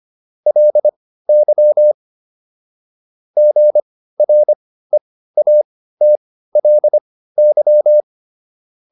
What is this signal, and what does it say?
Morse code, heard as a keyed tone, about 25 wpm, spaced as for 15 wpm, 600 Hz. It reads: LY GREATLY